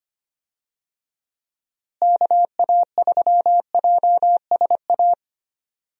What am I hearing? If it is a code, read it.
Morse code, heard as a keyed tone, about 25 wpm, 700 Hz. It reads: KA3JSA
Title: USA Call Signs - 6 Characters - Repeated 3x - 25wpm